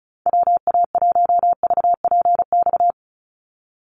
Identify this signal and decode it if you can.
Morse code, heard as a keyed tone, about 34 wpm, 700 Hz. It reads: WA1VPX
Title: USA Call Signs 35wpm